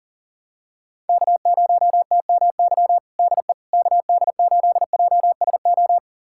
Morse code, heard as a keyed tone, about 37 wpm, 700 Hz. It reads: K0TMY DE KD8JSO
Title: USA Callsign DE Callsign 40wpm